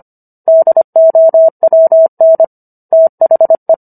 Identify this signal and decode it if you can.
Morse code, heard as a keyed tone, about 25 wpm, 650 Hz. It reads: DOWN THE